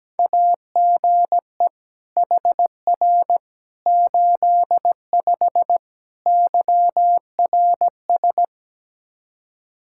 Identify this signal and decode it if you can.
Morse code, heard as a keyed tone, about 17 wpm, 700 Hz. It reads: AGE HR 85 YRS
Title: Practice QSO - Exchange 3 - More Info - 17wpm